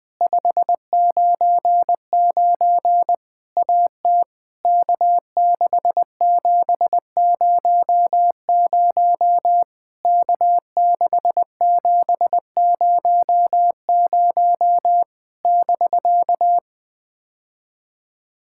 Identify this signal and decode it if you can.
Morse code, heard as a keyed tone, about 20 wpm, 700 Hz. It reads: 599 AT K6700 K6700 <BK>
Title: POTA QSO Practice - 20wpm